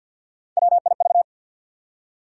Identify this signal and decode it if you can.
Morse code, heard as a keyed tone, about 53 wpm, 700 Hz. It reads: WIV
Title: USA Call Sign Postfixes - 3 Letters 50wpm